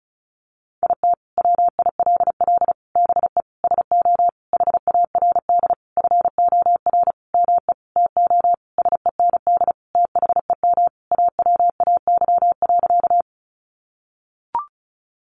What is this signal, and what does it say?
Morse code, heard as a keyed tone, about 34 wpm, 700 Hz. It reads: IT WILL BE SO HARD FOR ME TO SEND THEM AWAY.
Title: Sentences from Top 1500 Words 35wpm